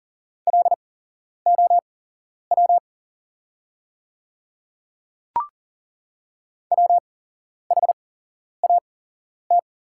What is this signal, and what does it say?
Morse code, heard as a keyed tone, about 40 wpm, spaced as for 11 wpm, 700 Hz. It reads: LOW WHAT
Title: 8x Character Spacing - Top 200 Words - 40wpm